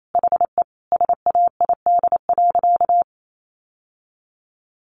Morse code, heard as a keyed tone, about 28 wpm, 700 Hz. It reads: HE SAID.
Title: Sentences from Top 1100 Words 28wpm